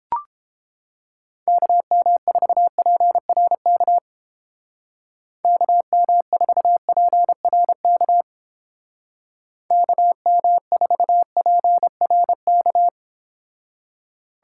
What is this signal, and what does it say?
Morse code, heard as a keyed tone, about 31 wpm, 700 Hz. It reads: KM4PRK KM4PRK KM4PRK